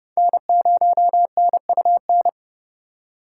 Morse code, heard as a keyed tone, about 38 wpm, 700 Hz. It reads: N0NUN